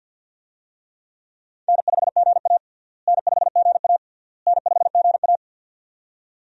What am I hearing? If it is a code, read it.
Morse code, heard as a keyed tone, about 51 wpm, 700 Hz. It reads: N5GA N5GA N5GA
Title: USA Call Signs - 4 Characters - Repeated 3x - 50wpm